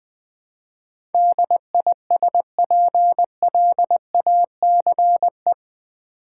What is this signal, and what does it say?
Morse code, heard as a keyed tone, about 20 wpm, 700 Hz. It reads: DISPLACE
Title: Build A Word - 3 Letters At A Time - 8-Letter Words - 20wpm